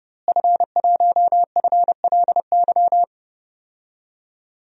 Morse code, heard as a keyed tone, about 30 wpm, 700 Hz. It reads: F1FLY
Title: DXCC-100 Call Signs - 5 Characters - Repeated 3x - 30wpm